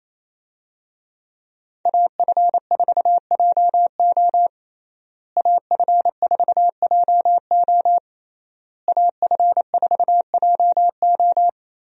Morse code, heard as a keyed tone, about 28 wpm, 700 Hz. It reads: AF4JO AF4JO AF4JO
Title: USA Call Signs - 5 Characters - Repeated 3x - 28wpm